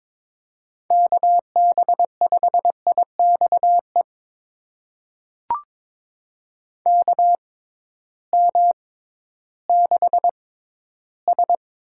Morse code, heard as a keyed tone, about 21 wpm, spaced as for 13 wpm, 700 Hz. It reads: KB5IXE KM6S
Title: USA Call Signs - 6 Characters - 6x Character Spacing - 22wpm